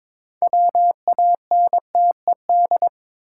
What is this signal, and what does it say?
Morse code, heard as a keyed tone, about 22 wpm, 700 Hz. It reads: WANTED
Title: Sets of 2 Words - New Word Drill - 22wpm